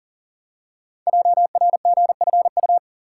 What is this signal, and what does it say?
Morse code, heard as a keyed tone, about 42 wpm, 690 Hz. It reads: JRGFU
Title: Code Groups - 5 Letters - 40wpm